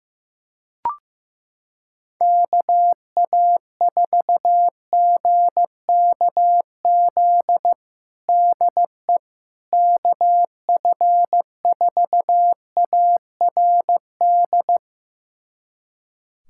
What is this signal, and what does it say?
Morse code, heard as a keyed tone, about 15 wpm, 700 Hz. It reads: KA4GKZ DE KF4ARD
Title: USA Callsign DE Callsign - 6 Characters - 15wpm